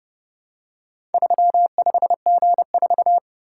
Morse code, heard as a keyed tone, about 29 wpm, 700 Hz. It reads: 35G4